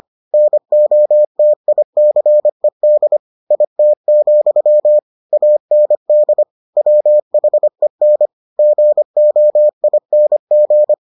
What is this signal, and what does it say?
Morse code, heard as a keyed tone, about 25 wpm, 600 Hz. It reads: NOTICED IT, AND WHEN GOING